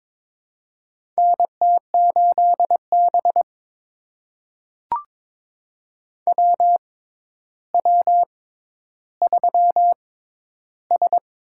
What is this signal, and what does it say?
Morse code, heard as a keyed tone, about 22 wpm, spaced as for 11 wpm, 700 Hz. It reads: NT8B WW3S
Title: USA Call Signs - 4 Characters - 6x Character Spacing - 22wpm